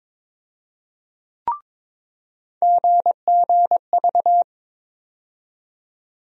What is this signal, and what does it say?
Morse code, heard as a keyed tone, about 22 wpm, 700 Hz. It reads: GGV
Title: USA Call Sign Postfixes - 3 Letters 22wpm